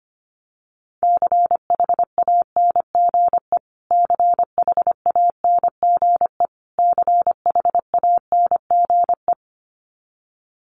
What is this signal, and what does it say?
Morse code, heard as a keyed tone, about 25 wpm, 700 Hz. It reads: CHANGE CHANGE CHANGE
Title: Top 200 Words Repeated 25wpm